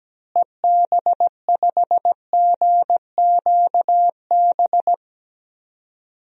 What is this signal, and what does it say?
Morse code, heard as a keyed tone, about 17 wpm, 700 Hz. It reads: EB5GQB
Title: DXCC-100 Call Signs - 6 Characters - Repeated 3x - 17wpm